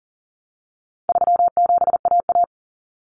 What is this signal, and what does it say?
Morse code, heard as a keyed tone, about 40 wpm, 700 Hz. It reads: 37AA
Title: Common Cut Numbers - 4-Digit 40wpm